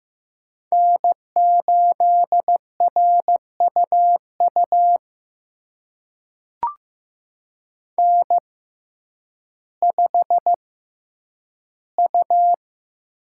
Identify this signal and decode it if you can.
Morse code, heard as a keyed tone, about 15 wpm, spaced as for 9 wpm, 700 Hz. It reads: N8RUU N5U
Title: USA Call Signs - 5 Characters - 6x Character Spacing - 15wpm